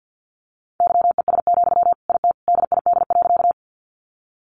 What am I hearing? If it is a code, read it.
Morse code, heard as a keyed tone, about 50 wpm, 700 Hz. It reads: YES, IT DID.
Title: Sentences from Top 500 Words 50wpm